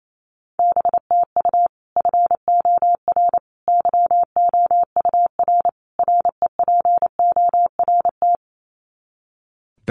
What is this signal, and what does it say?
Morse code, heard as a keyed tone, about 28 wpm, 700 Hz. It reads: BTU FOR YOUR REPORT
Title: Common QSO Phrases 28wpm